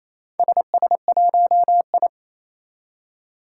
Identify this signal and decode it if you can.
Morse code, heard as a keyed tone, about 29 wpm, 700 Hz. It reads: SS1I
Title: Four Challenging Letter-Number Combinations 28wpm